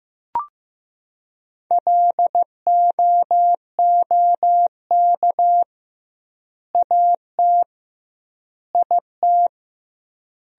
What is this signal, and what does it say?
Morse code, heard as a keyed tone, about 15 wpm, 700 Hz. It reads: LOOK AT IT
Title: 2x Word-Spacing - Sentences from Top 200 Words - 15wpm